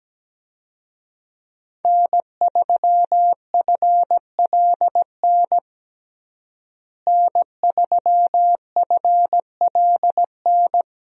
Morse code, heard as a keyed tone, about 17 wpm, 700 Hz. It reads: N3FLN N3FLN
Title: USA Call Signs - 5 Characters - Repeated 2x - 17wpm